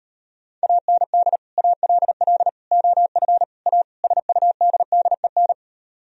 Morse code, heard as a keyed tone, about 38 wpm, 700 Hz. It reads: AND ALL OF A SUDDEN